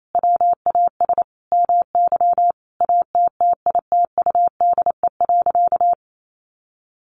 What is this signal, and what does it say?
Morse code, heard as a keyed tone, about 28 wpm, 700 Hz. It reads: WAS MY ATTITUDE.
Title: Sentences from Top 1300 Words 28wpm